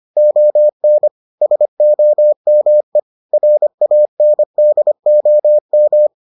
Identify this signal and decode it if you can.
Morse code, heard as a keyed tone, about 25 wpm, 600 Hz. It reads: ON SOME RANDOM